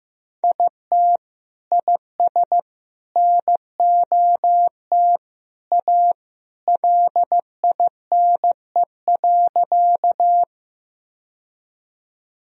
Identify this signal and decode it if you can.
Morse code, heard as a keyed tone, about 15 wpm, 700 Hz. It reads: IT IS NOT A LINE.
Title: Sentences from Top 1000 Words - Review Part 1 - 15wpm